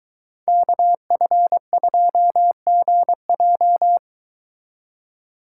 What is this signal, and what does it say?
Morse code, heard as a keyed tone, about 23 wpm, 700 Hz. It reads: KF2GJ